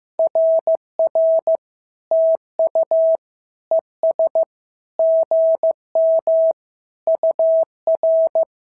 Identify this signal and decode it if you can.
Morse code, heard as a keyed tone, about 14 wpm, 650 Hz. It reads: RR TU ES GM UR
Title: SOTA QSO Practice 15wpm